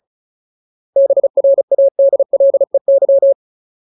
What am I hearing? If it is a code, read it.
Morse code, heard as a keyed tone, about 35 wpm, 550 Hz. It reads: BRADLEY
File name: Ultimate Word List / 35WPM Ultimate